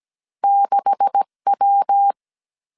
Morse code, heard as a keyed tone, about 17 wpm, 800 Hz. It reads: 6W
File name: 1500C S B 16wpm